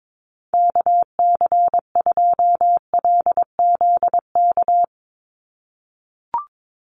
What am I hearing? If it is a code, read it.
Morse code, heard as a keyed tone, about 22 wpm, 700 Hz. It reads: KC2LZK
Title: USA Call Signs 22wpm